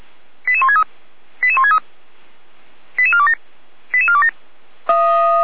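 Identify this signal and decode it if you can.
Five-tone selective call (ZVEI), heard as quick runs of five tones, tones of about 70 ms. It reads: 88163 88163 88427 88427
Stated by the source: ZVEI five-tone selective call received by SDR, logged by fire brigade feuerwehr-orlamuende.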